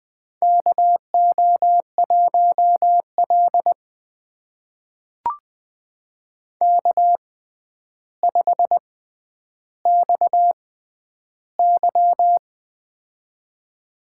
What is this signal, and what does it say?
Morse code, heard as a keyed tone, about 20 wpm, spaced as for 10 wpm, 700 Hz. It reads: KO1L K5XY
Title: USA Call Signs - 4 Characters - 6x Character Spacing - 20wpm